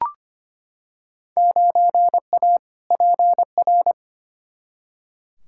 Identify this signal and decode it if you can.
Morse code, heard as a keyed tone, about 25 wpm, 700 Hz. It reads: 9A PR
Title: Field Day Exchanges Encore - 25wpm